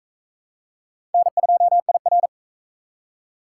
Morse code, heard as a keyed tone, about 38 wpm, 700 Hz. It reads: N2IR